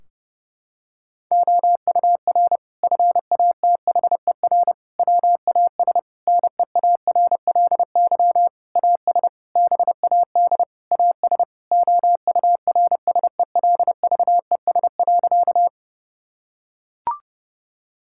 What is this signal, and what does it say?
Morse code, heard as a keyed tone, about 29 wpm, 700 Hz. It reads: OUR FATHER WAS NEARLY AS BAD AS OURSELVES.
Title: Sentences from Top 1600 Words 30wpm